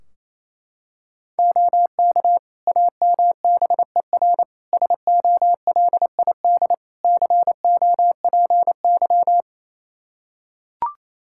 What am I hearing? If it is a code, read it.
Morse code, heard as a keyed tone, about 28 wpm, 700 Hz. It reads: OK AMBER SOLID COPY
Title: Practice QSO - Exchange 3 - More Info - 28wpm